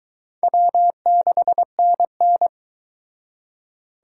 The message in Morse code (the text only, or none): W6NN